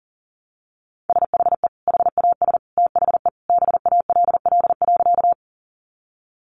Morse code, heard as a keyed tone, about 40 wpm, 700 Hz. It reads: SHE HAS THE BALL.
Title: Sentences from Top 900 Words 40wpm